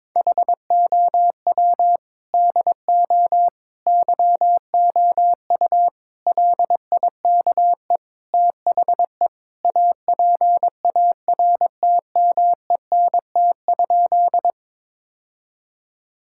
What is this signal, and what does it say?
Morse code, heard as a keyed tone, about 22 wpm, 700 Hz. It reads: HOW DO YOU LIKE THE APARTMENT?
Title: Sentences from Top 1000 Words - Review Part 6 - 22wpm